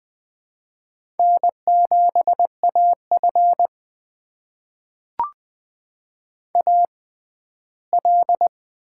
Morse code, heard as a keyed tone, about 20 wpm, spaced as for 14 wpm, 700 Hz. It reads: N7AF AL
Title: USA Call Signs - 4 Characters - 6x Character Spacing - 20wpm